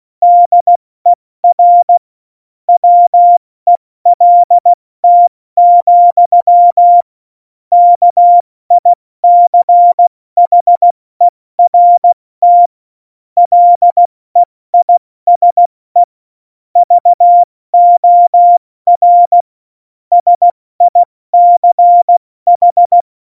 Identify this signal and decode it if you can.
Morse code, heard as a keyed tone, about 16 wpm, 700 Hz. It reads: DER WELT, KICHERT LEISE VOR SICH
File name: Morsekurs / Witze / Witz 08. Dezember 2020, 70 F80